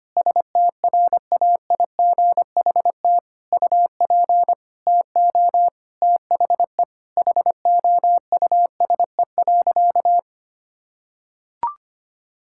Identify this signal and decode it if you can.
Morse code, heard as a keyed tone, about 25 wpm, 700 Hz. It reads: STRAIGHT UP TO THE HOUSE.
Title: Sentences from Top 1000 Words - Review Part 6 - 25wpm